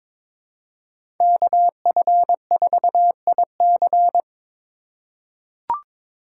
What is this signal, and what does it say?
Morse code, heard as a keyed tone, about 21 wpm, 700 Hz. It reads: KF4IC
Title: USA Call Signs - 5 Characters - 3x Character Spacing - 22wpm